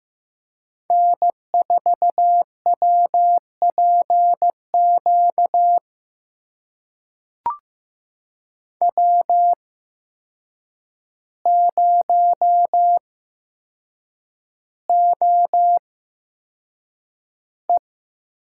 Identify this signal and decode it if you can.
Morse code, heard as a keyed tone, about 15 wpm, spaced as for 7 wpm, 700 Hz. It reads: N4WPQ W0OE
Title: USA Call Signs - 5 Characters - 8x Character Spacing - 15wpm